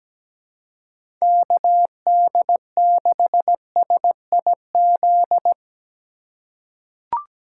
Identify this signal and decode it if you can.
Morse code, heard as a keyed tone, about 17 wpm, 700 Hz. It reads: KD6SIZ
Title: USA Call Signs - 6 Characters - 3x Character Spacing - 17wpm